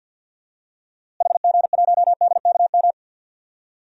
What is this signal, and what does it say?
Morse code, heard as a keyed tone, about 46 wpm, 700 Hz. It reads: HG1DKM
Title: DXCC-100 Call Signs - 6 Characters - Repeated 3x - 50wpm